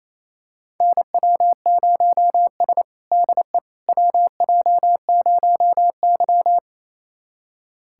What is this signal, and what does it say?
Morse code, heard as a keyed tone, about 28 wpm, 690 Hz. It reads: NW0S DE WJ0Y
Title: USA Callsign DE Callsign - 4 Characters - 28wpm